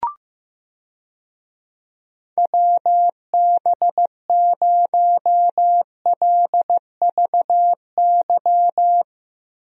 Morse code, heard as a keyed tone, about 15 wpm, 700 Hz. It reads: WB0LVY
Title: USA Call Signs - 6 Characters - Repeated 2x - 15wpm